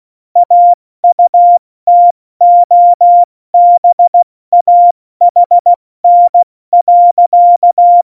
Morse code, heard as a keyed tone, about 16 wpm, 700 Hz. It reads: AUTOBAHN.
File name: Morsekurs / Witze / Witz 05. Februar 2019, 70 F80